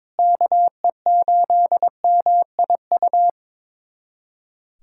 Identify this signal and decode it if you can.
Morse code, heard as a keyed tone, about 22 wpm, 700 Hz. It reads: KE8MIU
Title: USA Callsign DE Callsign - 6 Characters - 22wpm